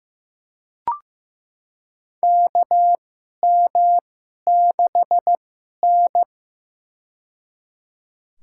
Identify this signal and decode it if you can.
Morse code, heard as a keyed tone, about 15 wpm, spaced as for 11 wpm, 700 Hz. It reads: KM6N
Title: USA Call Signs - 4 Characters - 2x Character Spacing - 15wpm